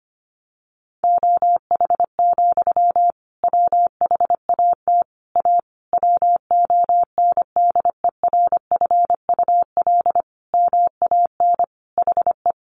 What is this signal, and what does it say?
Morse code, heard as a keyed tone, about 25 wpm, 700 Hz. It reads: OH, WHAT A WONDERFUL MAN HE
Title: Sentences from Top 1100 Words 25wpm